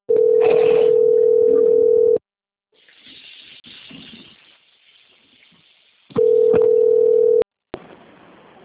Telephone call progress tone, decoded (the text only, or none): ringback tone